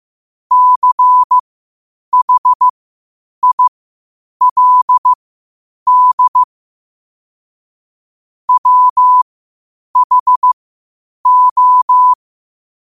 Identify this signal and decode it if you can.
Morse code, heard as a keyed tone, about 15 wpm, spaced as for 8 wpm, 1000 Hz. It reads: CHILD WHO
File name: qotdfiles / 8wpm